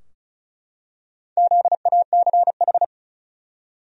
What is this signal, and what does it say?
Morse code, heard as a keyed tone, about 35 wpm, 700 Hz. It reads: ZACH
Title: USA Common Names - Max 4 Letters - 35wpm